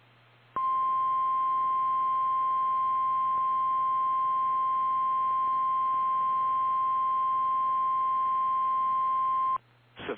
Emergency Alert System (SAME) tones (the 1050 Hz attention tone). It is heard from 0.5 to 9.6 s.
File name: eas / KKPRKXPN / KKPRKXPN